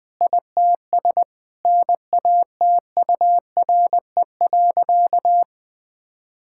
Morse code, heard as a keyed tone, about 20 wpm, 700 Hz. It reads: ITS NATURE.
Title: Sentences from Top 1000 Words - Review Part 6 - 20wpm